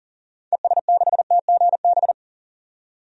Morse code, heard as a keyed tone, about 40 wpm, 700 Hz. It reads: ES6TGB
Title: DXCC-100 Call Signs - 6 Characters - 40wpm